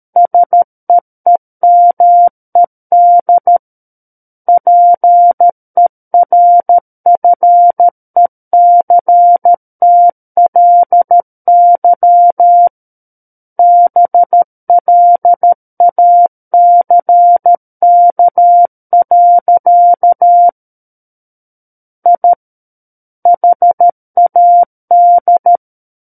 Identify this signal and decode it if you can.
Morse code, heard as a keyed tone, about 13 wpm, 700 Hz. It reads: SEEMED PERFECTLY BLACK. I HAD
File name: Book One / 04-Chapter 4 - 13 WPM